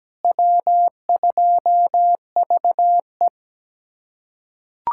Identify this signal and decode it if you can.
Morse code, heard as a keyed tone, about 17 wpm, 700 Hz. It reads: W2VE
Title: USA Callsign DE Callsign - 4 Characters - 17wpm